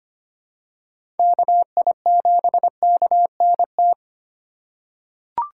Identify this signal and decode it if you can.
Morse code, heard as a keyed tone, about 25 wpm, 700 Hz. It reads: KI7KNT
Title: USA Call Signs - 6 Characters - 2x Character Spacing - 25wpm